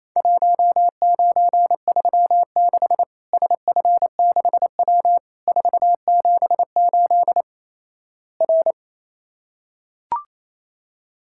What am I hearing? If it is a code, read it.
Morse code, heard as a keyed tone, about 27 wpm, 700 Hz. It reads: 1936 SF6W 478 R
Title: Worked All Europe DX Contest - QTC Exchange - 28wpm